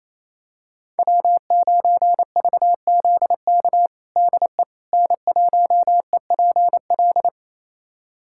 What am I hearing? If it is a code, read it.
Morse code, heard as a keyed tone, about 28 wpm, 700 Hz. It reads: W9VZK DE N1EPL
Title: USA Callsign DE Callsign - 5 Characters - 28wpm